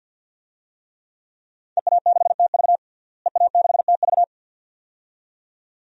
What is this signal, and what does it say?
Morse code, heard as a keyed tone, about 52 wpm, 700 Hz. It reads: EA6TV EA6TV
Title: DXCC-100 Call Signs - 5 Characters - Repeated 2x - 50wpm